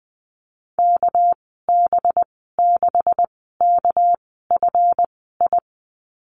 Morse code, heard as a keyed tone, about 20 wpm, spaced as for 14 wpm, 700 Hz. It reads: KB6KFI
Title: 231 Lesson 37g - Letter x - Call Signs 20wpm